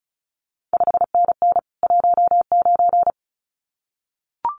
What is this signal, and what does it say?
Morse code, heard as a keyed tone, about 36 wpm, 700 Hz. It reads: 5NN 19
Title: CQ WW DX Contest Exchange Encore - 35wpm